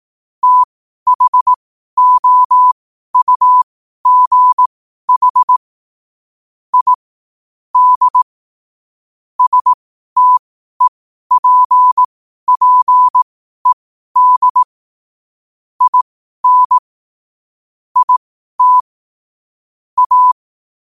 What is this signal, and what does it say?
Morse code, heard as a keyed tone, about 18 wpm, spaced as for 12 wpm, 1000 Hz. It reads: THOUGH I D STEPPED IN IT A
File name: qotdfiles / 12wpm